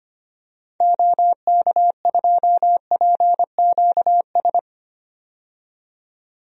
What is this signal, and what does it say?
Morse code, heard as a keyed tone, about 25 wpm, 700 Hz. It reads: OK2PQS
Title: DXCC-100 Call Signs - 6 Characters - 25wpm